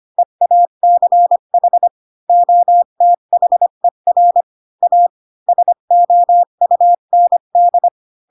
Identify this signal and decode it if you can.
Morse code, heard as a keyed tone, about 25 wpm, 700 Hz. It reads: EACH OTHER A SOUND